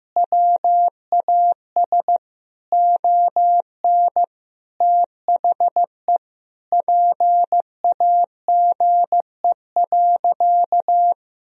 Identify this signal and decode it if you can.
Morse code, heard as a keyed tone, about 15 wpm, 700 Hz. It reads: WAS ON THE PAGE.
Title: Sentences from Top 1000 Words - Review Part 1 - 15wpm